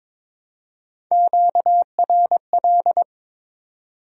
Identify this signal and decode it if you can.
Morse code, heard as a keyed tone, about 22 wpm, 700 Hz. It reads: QRL
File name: practice / 22wpm / Top 20 QSO Elements - 22wpm